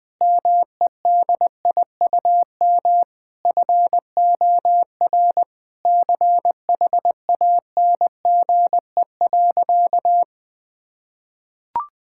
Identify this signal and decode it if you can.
Morse code, heard as a keyed tone, about 20 wpm, 700 Hz. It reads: MEDIUM FOR CHANGE.
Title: Sentences from Top 1000 Words - Review Part 6 - 20wpm